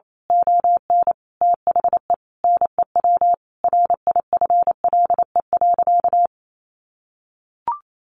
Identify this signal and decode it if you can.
Morse code, heard as a keyed tone, about 28 wpm, 700 Hz. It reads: ON THE NEW RIFLE.
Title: A Sound Of Thunder 28wpm